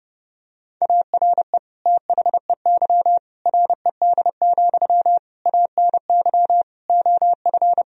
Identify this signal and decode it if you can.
Morse code, heard as a keyed tone, about 32 wpm, 700 Hz. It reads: ARE THEY RED, ANY OF
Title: Sentences from Top 1000 Words - Review Part 2 - 30wpm